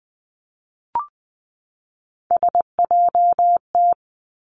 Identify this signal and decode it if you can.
Morse code, heard as a keyed tone, about 20 wpm, 700 Hz. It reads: SJT
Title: Three Challenging Letter-Number Combinations 20wpm